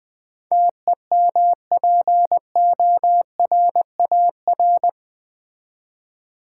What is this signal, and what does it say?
Morse code, heard as a keyed tone, about 20 wpm, 700 Hz. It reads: TEMPORAR
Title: Build A Word - 4 Letters At A Time - 11-Letter Words - 20wpm